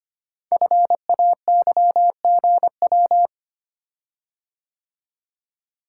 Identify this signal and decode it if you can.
Morse code, heard as a keyed tone, about 25 wpm, 700 Hz. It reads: FAYGW
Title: Code Groups - 5 Letters - 25wpm